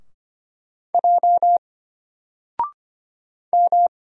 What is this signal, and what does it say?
Morse code, heard as a keyed tone, about 25 wpm, 700 Hz. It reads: J M
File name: Single-Letter-Number-25wpm